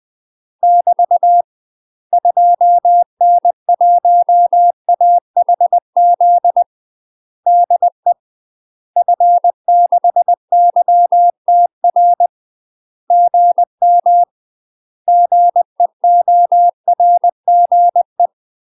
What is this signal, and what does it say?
Morse code, heard as a keyed tone, about 20 wpm, 700 Hz. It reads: <BT> 2N1AHZ DE F6YTR GM GEORGE
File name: UK MASTER QSOs TRAINING / 20WPM / UK MASTER QSOs 0003